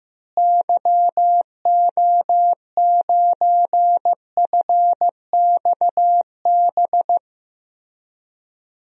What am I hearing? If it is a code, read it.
Morse code, heard as a keyed tone, about 15 wpm, 700 Hz. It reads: YO9FXB
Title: DXCC-100 Call Signs - 6 Characters - Repeated 3x - 15wpm